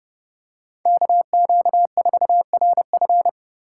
Morse code, heard as a keyed tone, about 31 wpm, 700 Hz. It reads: KQ4RF
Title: USA Call Signs - 5 Characters - 3x Character Spacing - 30wpm